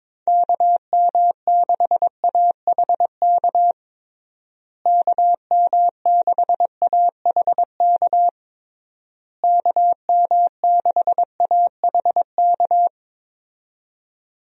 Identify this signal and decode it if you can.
Morse code, heard as a keyed tone, about 22 wpm, 700 Hz. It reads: KM6AHK KM6AHK KM6AHK
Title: USA Call Signs - 6 Characters - Repeated 3x - 22wpm